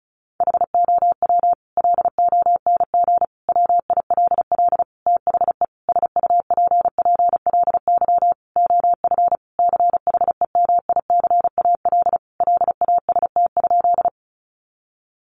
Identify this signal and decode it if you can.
Morse code, heard as a keyed tone, about 34 wpm, 700 Hz. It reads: HOW LONG WILL THE SUPPLY OF CHEMICAL LAST?
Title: Sentences from Top 1200 Words 35wpm